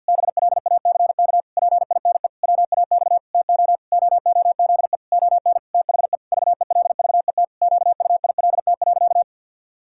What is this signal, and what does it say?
Morse code, heard as a keyed tone, about 52 wpm, 700 Hz. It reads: BLACK PINE WAX TO OOZE ON THE VELVET QUILT.
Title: Pangram Challenge v2 - 50wpm